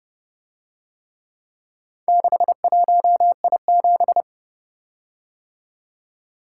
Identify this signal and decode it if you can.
Morse code, heard as a keyed tone, about 30 wpm, 700 Hz. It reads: 61I7